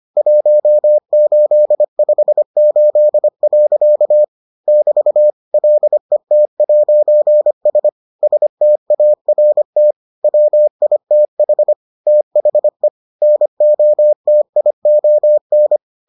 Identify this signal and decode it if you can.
Morse code, heard as a keyed tone, about 25 wpm, 600 Hz. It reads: 1858. <BT> LET'S START WITH THE NOTION